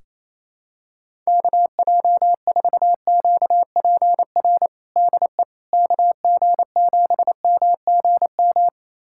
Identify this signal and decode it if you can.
Morse code, heard as a keyed tone, about 28 wpm, 700 Hz. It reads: KJ4QPR DE KG7MGM
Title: USA Callsign DE Callsign 28wpm